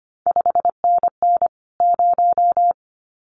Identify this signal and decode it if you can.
Morse code, heard as a keyed tone, about 25 wpm, 700 Hz. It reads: 5NN 0
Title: CQ WW DX Contest Exchange Encore - 25wpm